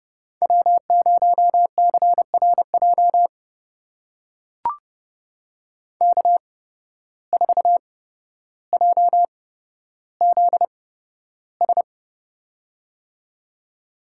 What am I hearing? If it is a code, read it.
Morse code, heard as a keyed tone, about 32 wpm, spaced as for 13 wpm, 700 Hz. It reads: W0CRJ K4JZS
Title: USA Call Signs - 5 Characters - 8x Character Spacing - 30wpm